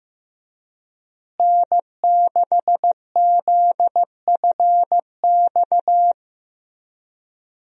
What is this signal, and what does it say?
Morse code, heard as a keyed tone, about 15 wpm, 700 Hz. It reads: N6ZFX
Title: USA Call Signs - 5 Characters - Repeated 3x - 15wpm